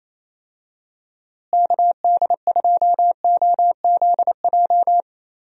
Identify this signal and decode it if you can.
Morse code, heard as a keyed tone, about 28 wpm, 700 Hz. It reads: KD2OZJ
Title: USA Call Signs - 6 Characters - Repeated 2x - 28wpm